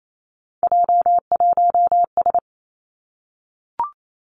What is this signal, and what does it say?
Morse code, heard as a keyed tone, about 28 wpm, 700 Hz. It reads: J1S
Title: Three Challenging Letter-Number Combinations 28wpm